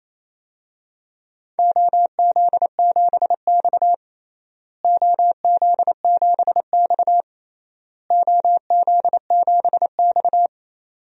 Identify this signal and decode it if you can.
Morse code, heard as a keyed tone, about 28 wpm, 700 Hz. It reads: OZ7X OZ7X OZ7X
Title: DXCC-100 Call Signs - 4 Characters - Repeated 3x - 28wpm